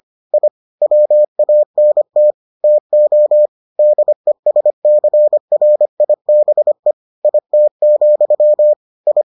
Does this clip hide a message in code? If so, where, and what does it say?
Yes: Morse code from 0.3 to 9.2 s, heard as a keyed tone, about 25 wpm, 600 Hz. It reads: I WANT TO DESCRIBE IT, I